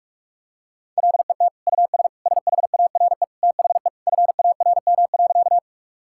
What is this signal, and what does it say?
Morse code, heard as a keyed tone, about 41 wpm, 700 Hz. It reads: LET US SHARE THE FARM.
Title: Sentences from Top 1000 Words - Review Part 6 - 45wpm